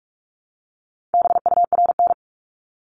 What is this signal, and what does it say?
Morse code, heard as a keyed tone, about 46 wpm, 700 Hz. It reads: BURN
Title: Multi-letter Phonemes Repeated 3x 45wpm